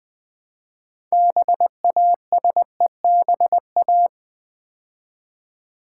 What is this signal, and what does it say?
Morse code, heard as a keyed tone, about 20 wpm, 700 Hz. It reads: BASEBA
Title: Build A Word - 3 Letters At A Time - 8-Letter Words - 20wpm